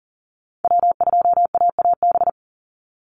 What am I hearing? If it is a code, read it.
Morse code, heard as a keyed tone, about 40 wpm, 700 Hz. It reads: W2AAB
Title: Call Signs Repeated 2x - 40wpm